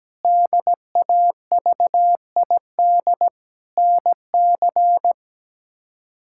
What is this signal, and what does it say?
Morse code, heard as a keyed tone, about 17 wpm, 700 Hz. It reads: DAVID NC
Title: K1USN SST QSO Practice - 17wpm